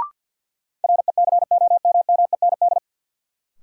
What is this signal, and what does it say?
Morse code, heard as a keyed tone, about 53 wpm, 700 Hz. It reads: RECOMMEND
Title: Sentences from Top 1200 Words - New Word Drill - 50wpm